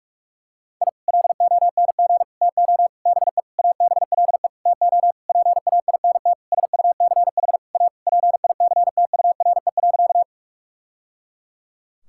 Morse code, heard as a keyed tone, about 47 wpm, 700 Hz. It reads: I LONG TO BE ABLE TO PAINT SUCH A PICTURE.
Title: Sentences from Top 1000 Words - Review Part 6 - 45wpm